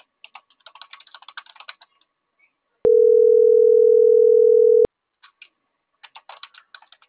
Ringback tone (North America), 440 + 480 Hz.